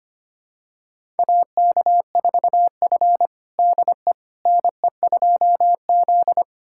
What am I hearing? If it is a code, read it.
Morse code, heard as a keyed tone, about 25 wpm, 700 Hz. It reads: AK4F DE NE2Z
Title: USA Callsign DE Callsign - 4 Characters - 25wpm